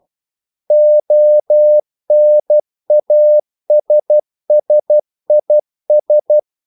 Morse code, heard as a keyed tone, about 12 wpm, 600 Hz. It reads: ONASSIS
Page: English quotations in CW (telegraphy - morse)